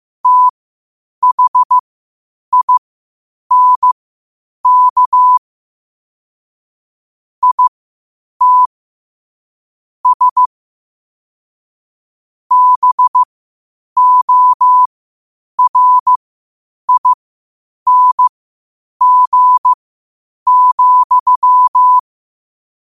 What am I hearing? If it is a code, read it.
Morse code, heard as a keyed tone, about 15 wpm, spaced as for 8 wpm, 1000 Hz. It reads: THINK IT S BORING,
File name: qotdfiles / 8wpm